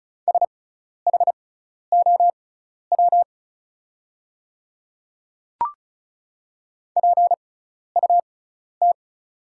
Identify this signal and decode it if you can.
Morse code, heard as a keyed tone, about 36 wpm, spaced as for 12 wpm, 700 Hz. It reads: SHOW PUT
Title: 6x Character Spacing - Top 200 Words - 35wpm